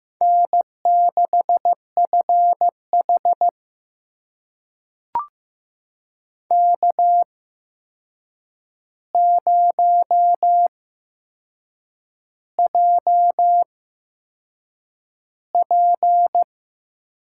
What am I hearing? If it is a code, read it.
Morse code, heard as a keyed tone, about 15 wpm, spaced as for 6 wpm, 700 Hz. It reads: N6FH K0JP
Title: USA Call Signs - 4 Characters - 8x Character Spacing - 15wpm